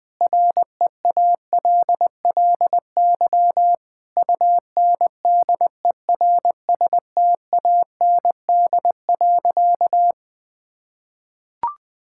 Morse code, heard as a keyed tone, about 20 wpm, 700 Hz. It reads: REALLY UNDERSTAND.
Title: Sentences with 4-Word Phrases - 20wpm